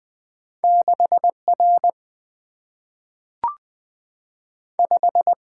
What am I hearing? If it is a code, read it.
Morse code, heard as a keyed tone, about 20 wpm, 700 Hz. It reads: 6 R 5
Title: Single-Letter Single-Number 20wpm